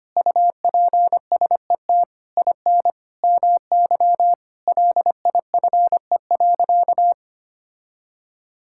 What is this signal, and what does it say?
Morse code, heard as a keyed tone, about 25 wpm, 700 Hz. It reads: UPSET IN MY LIFE.
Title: Sentences from Top 1000 Words - Review Part 5 - 25wpm